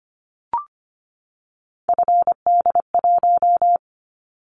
Morse code, heard as a keyed tone, about 25 wpm, 700 Hz. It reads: FD1
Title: Callsign Prefixes for Top 50 DXCC - 25wpm